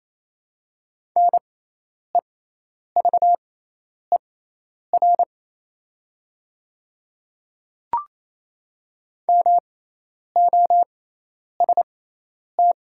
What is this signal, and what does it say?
Morse code, heard as a keyed tone, about 28 wpm, spaced as for 10 wpm, 700 Hz. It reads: NEVER MOST
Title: 6x Character Spacing - Top 300 Words - 28wpm